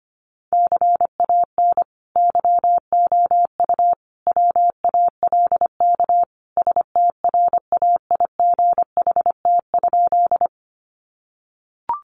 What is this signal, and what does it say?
Morse code, heard as a keyed tone, about 25 wpm, 700 Hz. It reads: CAN YOU WALK STRAIGHT?
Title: Sentences from Top 1000 Words 25wpm